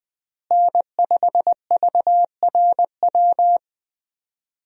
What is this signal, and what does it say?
Morse code, heard as a keyed tone, about 21 wpm, 700 Hz. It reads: N5VRW
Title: USA Call Signs - 5 Characters - Repeated 3x - 20wpm